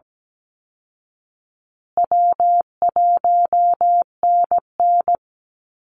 Morse code, heard as a keyed tone, about 17 wpm, 700 Hz. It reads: W1NN
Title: Call Signs Repeated 3x - 17wpm